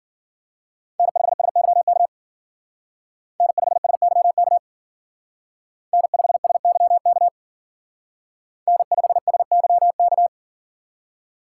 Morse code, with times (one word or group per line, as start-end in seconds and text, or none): 1.0-2.1 N5SYK
3.4-4.6 N5SYK
5.9-7.3 N5SYK
8.7-10.3 N5SYK